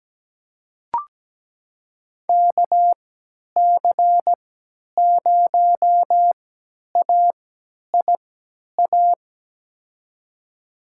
Morse code, heard as a keyed tone, about 17 wpm, spaced as for 10 wpm, 700 Hz. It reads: KC0AIA
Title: USA Call Signs - 3x Character Spacing - 17wpm